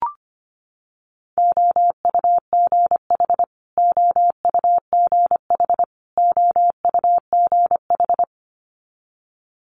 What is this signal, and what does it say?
Morse code, heard as a keyed tone, about 25 wpm, 700 Hz. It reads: OUGH OUGH OUGH
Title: Multi-letter Phonemes Repeated 3x 25wpm